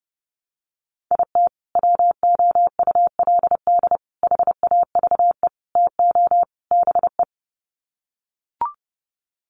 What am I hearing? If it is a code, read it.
Morse code, heard as a keyed tone, about 28 wpm, 700 Hz. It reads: IT WOULD HAVE TO BE
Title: Sets of 5 Words Encore 30wpm